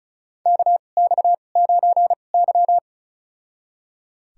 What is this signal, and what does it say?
Morse code, heard as a keyed tone, about 36 wpm, spaced as for 25 wpm, 700 Hz. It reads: KX9Y
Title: USA Call Signs - 4 Characters - 2x Character Spacing - 35wpm